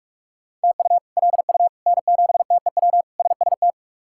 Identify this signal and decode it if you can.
Morse code, heard as a keyed tone, about 46 wpm, 700 Hz. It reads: TU LU N7TEW SST
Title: K1USN SST QSO Practice - 45wpm